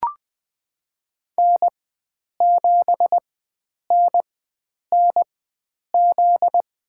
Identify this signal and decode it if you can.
Morse code, heard as a keyed tone, about 20 wpm, spaced as for 9 wpm, 700 Hz. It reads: N7NNZ
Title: USA Call Signs - 4x Character Spacing - 20wpm